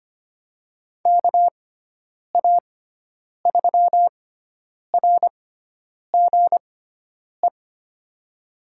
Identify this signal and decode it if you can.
Morse code, heard as a keyed tone, about 25 wpm, spaced as for 9 wpm, 700 Hz. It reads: KA3RGE
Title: USA Call Signs - 6x Character Spacing - 25wpm